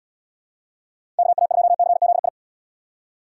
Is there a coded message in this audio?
Yes: Morse code from 1.2 to 2.3 s, heard as a keyed tone, about 78 wpm, 700 Hz. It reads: KA2CZI